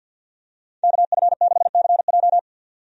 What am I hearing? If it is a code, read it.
Morse code, heard as a keyed tone, about 53 wpm, 700 Hz. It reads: KF6CJ
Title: USA Call Signs - 5 Characters - 50wpm